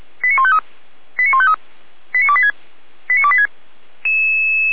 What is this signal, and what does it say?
Five-tone selective call (ZVEI), heard as quick runs of five tones, tones of about 70 ms. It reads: 89153 89153 89286 89286